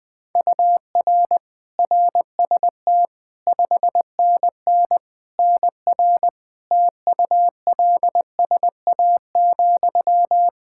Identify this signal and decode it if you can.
Morse code, heard as a keyed tone, about 20 wpm, 700 Hz. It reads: UR RST 5NN NR TULSA,
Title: Practice QSO - Exchange 1 - Beginning - 20wpm